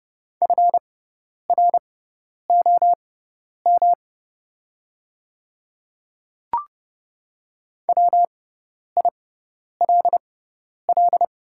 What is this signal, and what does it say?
Morse code, heard as a keyed tone, about 31 wpm, spaced as for 10 wpm, 700 Hz. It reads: FROM WILL
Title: 6x Character Spacing - Top 100 Words - 30wpm